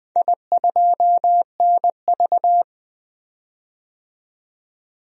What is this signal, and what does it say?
Morse code, heard as a keyed tone, about 20 wpm, 700 Hz. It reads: I2NV
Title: DXCC-100 Call Signs - 4 Characters - 20wpm